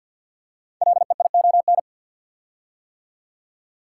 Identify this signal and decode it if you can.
Morse code, heard as a keyed tone, about 50 wpm, 700 Hz. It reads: LEION
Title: Code Groups - 5 Letter:Numbers - 50wpm